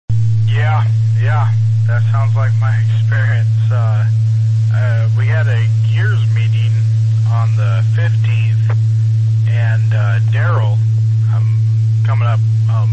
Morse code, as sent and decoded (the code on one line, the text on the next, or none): none